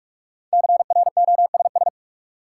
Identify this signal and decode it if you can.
Morse code, heard as a keyed tone, about 46 wpm, 700 Hz. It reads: CROSS
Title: Sentences from Top 300 Words - New Word Drill - 45wpm